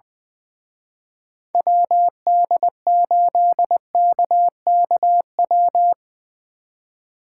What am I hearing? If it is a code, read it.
Morse code, heard as a keyed tone, about 22 wpm, 700 Hz. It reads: WD8KKW
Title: Speed Racing - USA 2x3 Call Signs 20wpm